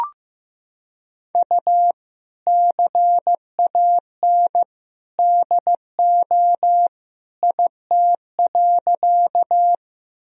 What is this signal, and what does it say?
Morse code, heard as a keyed tone, about 15 wpm, 700 Hz. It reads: U CAN DO IT.
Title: Sentences from Top 100 Words and Abbreviations 15wpm